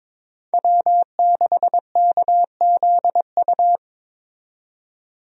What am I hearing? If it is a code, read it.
Morse code, heard as a keyed tone, about 22 wpm, 700 Hz. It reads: W6KZU
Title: USA Call Signs - 5 Characters - 22wpm